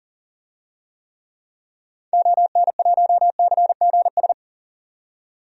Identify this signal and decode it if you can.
Morse code, heard as a keyed tone, about 40 wpm, 700 Hz. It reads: ON1CGS
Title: DXCC-100 Call Signs - 6 Characters - 40wpm